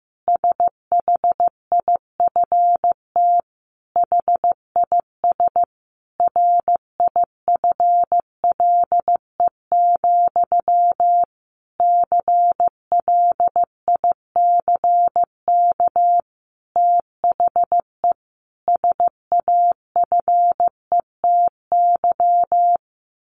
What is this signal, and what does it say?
Morse code, heard as a keyed tone, about 15 wpm, 700 Hz. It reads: SHIFT HIS RIFLE, CLICK THE SAFETY
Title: A Sound Of Thunder 15wpm